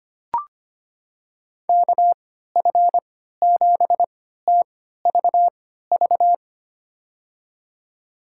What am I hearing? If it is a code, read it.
Morse code, heard as a keyed tone, about 25 wpm, spaced as for 14 wpm, 700 Hz. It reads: KF7TVV
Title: USA Call Signs - 3x Character Spacing - 25wpm